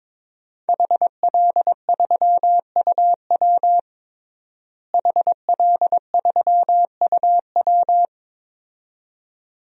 Morse code, heard as a keyed tone, about 22 wpm, 700 Hz. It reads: HL3UW HL3UW
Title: DXCC-100 Call Signs - 5 Characters - Repeated 2x - 22wpm